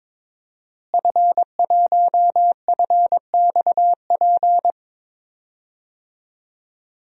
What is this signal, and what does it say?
Morse code, heard as a keyed tone, about 22 wpm, 700 Hz. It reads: F1FXP
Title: DXCC-100 Call Signs - 5 Characters - Repeated 3x - 22wpm